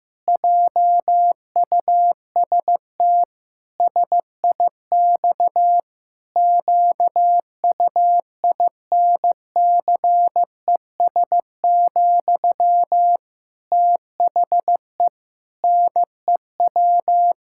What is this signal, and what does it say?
Morse code, heard as a keyed tone, about 15 wpm, 700 Hz. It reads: JUST SIX QUINCES, THE NEW
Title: Pangram Challenge v2 - 15wpm